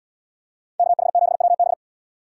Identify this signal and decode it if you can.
Morse code, heard as a keyed tone, about 76 wpm, 700 Hz. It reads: KH7WB